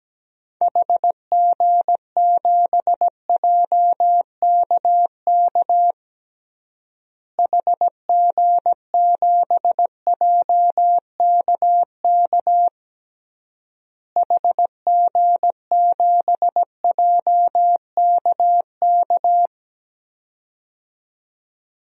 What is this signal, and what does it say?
Morse code, heard as a keyed tone, about 17 wpm, 700 Hz. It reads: HG7JKK HG7JKK HG7JKK
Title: DXCC-100 Call Signs - 6 Characters - Repeated 3x - 17wpm